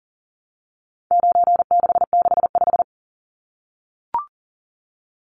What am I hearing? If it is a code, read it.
Morse code, heard as a keyed tone, about 39 wpm, 700 Hz. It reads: 9665
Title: Numbers - 4-Digit 40wpm